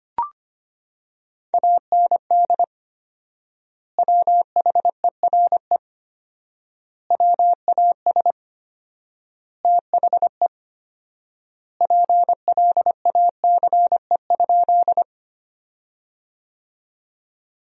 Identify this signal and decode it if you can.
Morse code, heard as a keyed tone, about 25 wpm, 700 Hz. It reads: AND WHERE WAS THE PLACE?
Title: 4x Word-Spacing - Sentences from Top 200 Words - 25wpm